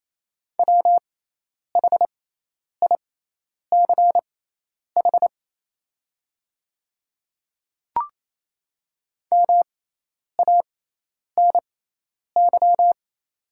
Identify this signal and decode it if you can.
Morse code, heard as a keyed tone, about 28 wpm, spaced as for 10 wpm, 700 Hz. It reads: WHICH MANY
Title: 6x Character Spacing - Top 200 Words - 28wpm